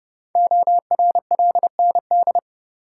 Morse code, heard as a keyed tone, about 30 wpm, 690 Hz. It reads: ORLND
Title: Code Groups - 5 Letters - 30wpm